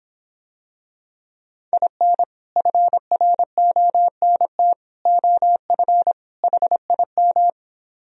Morse code, heard as a keyed tone, about 25 wpm, 700 Hz. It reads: IN FRONT OF HIM